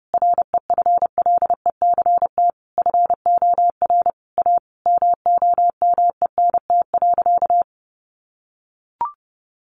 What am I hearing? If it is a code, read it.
Morse code, heard as a keyed tone, about 30 wpm, 700 Hz. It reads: REFLECT FOR A MOMENT.
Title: Sentences from Top 1500 Words 30wpm